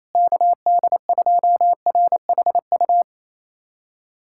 Morse code, heard as a keyed tone, about 27 wpm, 700 Hz. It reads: KD2RHU
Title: USA Callsign DE Callsign - 6 Characters - 28wpm